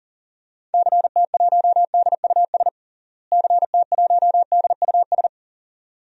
Morse code, heard as a keyed tone, about 37 wpm, 700 Hz. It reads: CT1DUS CT1DUS
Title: DXCC-100 Call Signs - 6 Characters - Repeated 2x - 40wpm